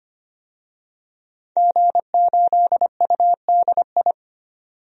Morse code, heard as a keyed tone, about 25 wpm, 700 Hz. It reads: G8UDI
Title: DXCC-100 Call Signs - 5 Characters - 25wpm